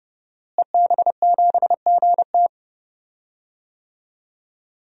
Morse code, heard as a keyed tone, about 30 wpm, 700 Hz. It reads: EB7GT
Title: DXCC-100 Call Signs - 5 Characters - Repeated 2x - 30wpm